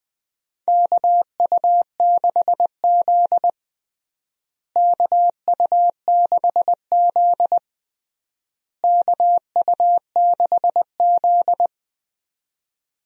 Morse code, heard as a keyed tone, about 20 wpm, 700 Hz. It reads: KU6Z KU6Z KU6Z
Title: USA Call Signs - 4 Characters - Repeated 3x - 20wpm